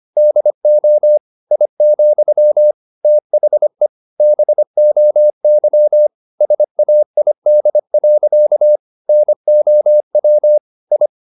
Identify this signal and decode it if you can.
Morse code, heard as a keyed tone, about 25 wpm, 600 Hz. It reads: DO I, THE BOY SAID. NOW I
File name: the old man and the sea part 1 25wpm